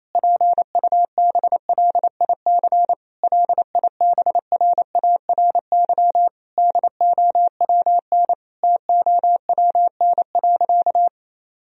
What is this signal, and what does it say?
Morse code, heard as a keyed tone, about 28 wpm, 700 Hz. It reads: PUBLIC LIBRARY DOWN TOWN.
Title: Sentences from Top 1000 Words - Review Part 4 - 28wpm